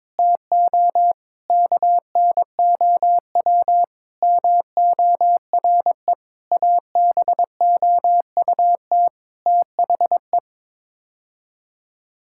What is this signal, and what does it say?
Morse code, heard as a keyed tone, about 22 wpm, 700 Hz. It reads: TO KNOW MORE ABOUT THE
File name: practice / 22wpm / Sets of 8 Words - 22wpm